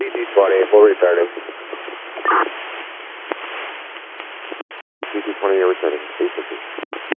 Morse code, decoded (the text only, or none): none